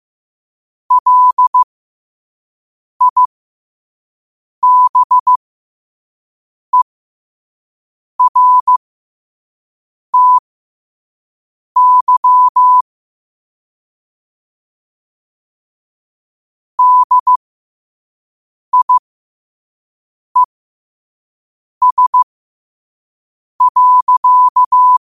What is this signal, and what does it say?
Morse code, heard as a keyed tone, about 15 wpm, spaced as for 5 wpm, 1000 Hz. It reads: LIBERTY DIES.
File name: qotdfiles / 5wpm